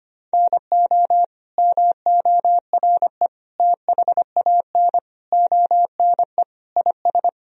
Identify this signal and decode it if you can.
Morse code, heard as a keyed tone, about 25 wpm, 700 Hz. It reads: NO MORE THAN ONE IS
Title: Sentences from Top 1000 Words - Review Part 2 - 25wpm